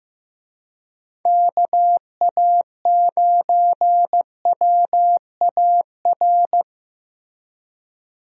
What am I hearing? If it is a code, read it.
Morse code, heard as a keyed tone, about 15 wpm, 700 Hz. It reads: KA9WAR
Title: USA Call Signs - 6 Characters - Repeated 2x - 15wpm